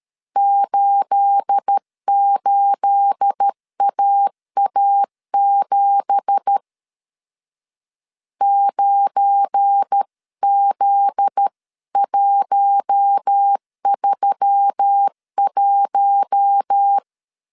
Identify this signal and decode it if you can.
Morse code, heard as a keyed tone, about 13 wpm, 800 Hz. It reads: 88AA7 9Z131
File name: CW / DET / 1 9 1500C S A 12wpm 3800